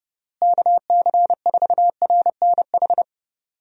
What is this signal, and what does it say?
Morse code, heard as a keyed tone, about 32 wpm, 700 Hz. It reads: KC4RNH
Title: USA Call Signs - 6 Characters - Repeated 3x - 30wpm